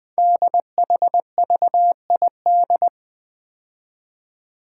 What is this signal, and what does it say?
Morse code, heard as a keyed tone, about 20 wpm, 700 Hz. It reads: DHVID
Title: Code Groups - 5 Letters - 20wpm